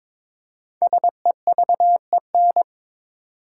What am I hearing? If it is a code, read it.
Morse code, heard as a keyed tone, about 22 wpm, 700 Hz. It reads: SEVEN
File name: 5 Letter Words - First 3 Letters Given - 22wpm